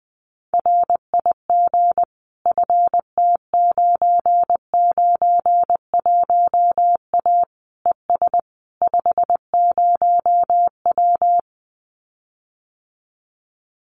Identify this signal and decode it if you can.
Morse code, heard as a keyed tone, about 20 wpm, 700 Hz. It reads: RIG FT991A ES 50W
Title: Common QSO Phrases Encore 20wpm